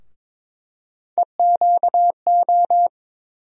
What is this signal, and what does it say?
Morse code, heard as a keyed tone, about 21 wpm, 700 Hz. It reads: EQO
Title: USA Call Sign Postfixes - 3 Letters 22wpm